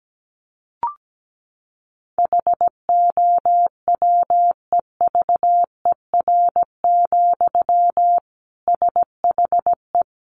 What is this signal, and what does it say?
Morse code, heard as a keyed tone, about 17 wpm, 700 Hz. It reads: HOWEVER, SHE
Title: Sentences from Top 700 Words 17wpm